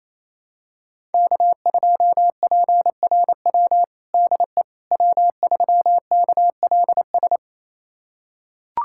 Morse code, heard as a keyed tone, about 27 wpm, 700 Hz. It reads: K2PRW DE W3KLS
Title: USA Callsign DE Callsign - 5 Characters - 28wpm